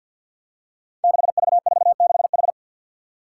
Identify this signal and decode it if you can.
Morse code, heard as a keyed tone, about 51 wpm, 700 Hz. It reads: 6V46H
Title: Five Challenging Letter-Number Combinations 50wpm